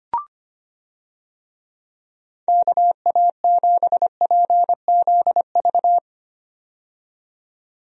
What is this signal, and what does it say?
Morse code, heard as a keyed tone, about 25 wpm, 700 Hz. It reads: KA7PZV
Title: USA Call Signs - 6 Characters - 25wpm